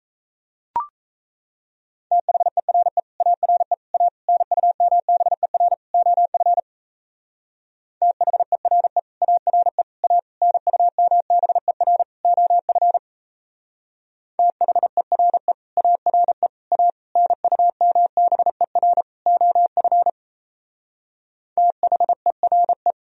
Morse code, heard as a keyed tone, about 36 wpm, 700 Hz. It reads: THERE ARE A NUMBER OF THERE ARE A NUMBER OF THERE ARE A NUMBER OF THERE